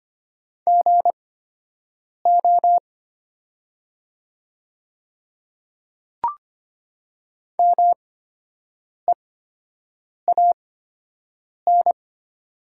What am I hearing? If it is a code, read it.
Morse code, heard as a keyed tone, about 25 wpm, spaced as for 7 wpm, 700 Hz. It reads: GO MEAN
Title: 8x Character Spacing - Top 200 Words - 25wpm